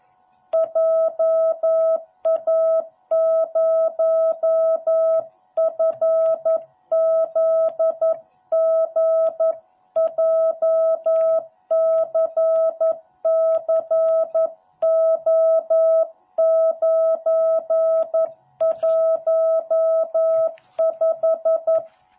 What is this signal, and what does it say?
Morse code, heard as a keyed tone, about 11 wpm, 660 Hz. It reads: JA0FZG JCCO915